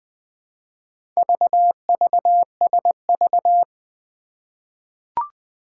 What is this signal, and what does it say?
Morse code, heard as a keyed tone, about 20 wpm, 700 Hz. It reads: VVSV
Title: Four Challenging Letter-Number Combinations 20wpm